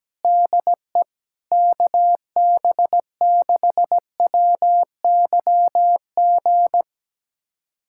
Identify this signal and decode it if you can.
Morse code, heard as a keyed tone, about 17 wpm, 700 Hz. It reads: DE KB6WYG
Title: USA Callsign DE Callsign - 6 Characters - 17wpm